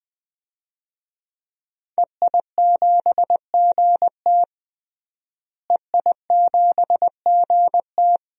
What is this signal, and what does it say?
Morse code, heard as a keyed tone, about 20 wpm, 700 Hz. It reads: EI7GT EI7GT
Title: DXCC-100 Call Signs - 5 Characters - Repeated 3x - 20wpm